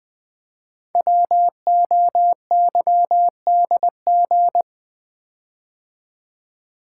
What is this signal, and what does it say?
Morse code, heard as a keyed tone, about 20 wpm, 700 Hz. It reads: WOYDG
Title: Code Groups - 5 Letters - 20wpm